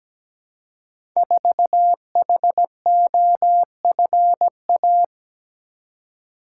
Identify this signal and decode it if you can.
Morse code, heard as a keyed tone, about 17 wpm, 700 Hz. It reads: 4HOFA
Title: Code Groups - 5 Letter:Numbers - 17wpm